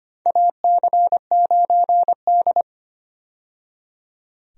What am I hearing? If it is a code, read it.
Morse code, heard as a keyed tone, about 25 wpm, 700 Hz. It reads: AC9D
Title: USA Call Signs - 4 Characters - 25wpm